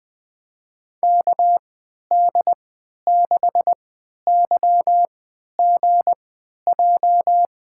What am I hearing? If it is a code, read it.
Morse code, heard as a keyed tone, about 20 wpm, spaced as for 11 wpm, 700 Hz. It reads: KD6YGJ
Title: USA Call Signs - 6 Characters - 3x Character Spacing - 20wpm